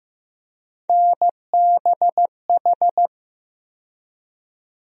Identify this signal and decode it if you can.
Morse code, heard as a keyed tone, about 15 wpm, 700 Hz. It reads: NBH
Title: USA Call Sign Postfixes - 3 Letters 15wpm